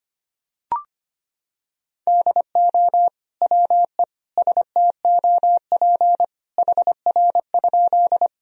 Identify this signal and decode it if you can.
Morse code, heard as a keyed tone, about 25 wpm, 700 Hz. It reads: DO WE STOP HR?
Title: Sentences from Top 300 Words and Abbreviations 25wpm